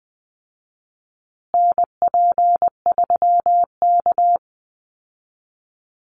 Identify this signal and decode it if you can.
Morse code, heard as a keyed tone, about 20 wpm, 700 Hz. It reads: NP3K
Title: Call Signs Repeated 3x - 20wpm